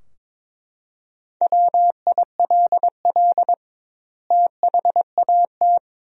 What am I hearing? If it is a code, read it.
Morse code, heard as a keyed tone, about 22 wpm, 700 Hz. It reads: WILL THAT
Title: 2x Word-Spacing - Sentences from Top 100 Words - 22wpm